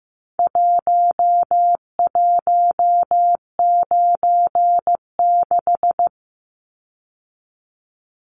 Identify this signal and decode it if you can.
Morse code, heard as a keyed tone, about 15 wpm, 700 Hz. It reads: 1196
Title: Numbers - Contest Style - 4-Digit 15wpm